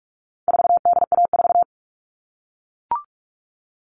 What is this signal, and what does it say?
Morse code, heard as a keyed tone, about 44 wpm, 700 Hz. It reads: 4DA4
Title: All Cut Numbers - 4-Digit 45wpm